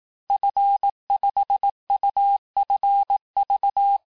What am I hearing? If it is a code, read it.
Morse code, heard as a keyed tone, about 18 wpm, 800 Hz. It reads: F5UFV
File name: cours-du-lundi files / cours-1588-FIN-de-QSO-vitesse-17MPM